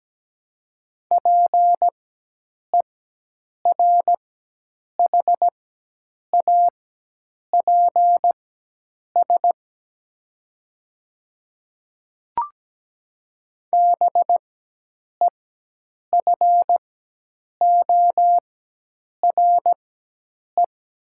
Morse code, heard as a keyed tone, about 17 wpm, spaced as for 8 wpm, 700 Hz. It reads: PERHAPS BEFORE
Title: 4x Character Spacing - Top 500 Words - 17wpm